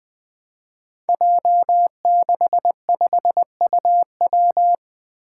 Morse code, heard as a keyed tone, about 20 wpm, 700 Hz. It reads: J65UW
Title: Five Challenging Letter-Number Combinations 20wpm